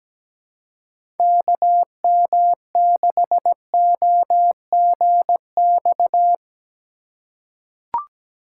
Morse code, heard as a keyed tone, about 17 wpm, 700 Hz. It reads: KM6OGX
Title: USA Call Signs - 6 Characters - 2x Character Spacing - 17wpm